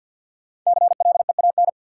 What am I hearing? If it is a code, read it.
Morse code, heard as a keyed tone, about 52 wpm, 700 Hz. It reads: CLEAN
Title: Five Letter Words - 50wpm